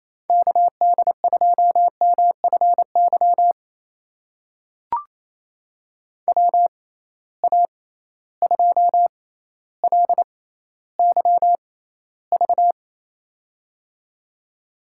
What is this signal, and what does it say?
Morse code, heard as a keyed tone, about 28 wpm, spaced as for 14 wpm, 700 Hz. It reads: KD2MFY WA2LYV
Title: USA Call Signs - 6 Characters - 6x Character Spacing - 28wpm